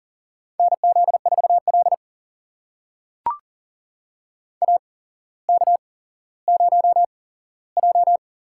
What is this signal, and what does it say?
Morse code, heard as a keyed tone, about 39 wpm, spaced as for 17 wpm, 700 Hz. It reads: NZ4L AK0J
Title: USA Call Signs - 4 Characters - 8x Character Spacing - 40wpm